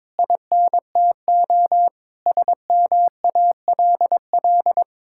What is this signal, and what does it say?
Morse code, heard as a keyed tone, about 22 wpm, 700 Hz. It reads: INTO SMALL
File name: Sentences with 9-Word Phrases - 22wpm